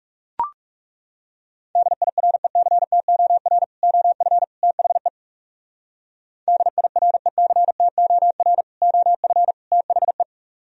Morse code, heard as a keyed tone, about 34 wpm, 700 Hz. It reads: DIRECTOR OF THE DIRECTOR OF THE
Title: Speed-Racing - Sets of 3 Words 30wpm